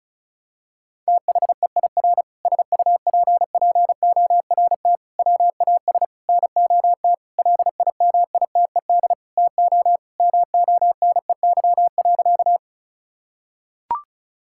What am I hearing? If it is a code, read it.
Morse code, heard as a keyed tone, about 34 wpm, 700 Hz. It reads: THEIR SUPPORT WAS NOT LIMITED TO MONEY.
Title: Sentences from Top 1000 Words - Review Part 6 - 35wpm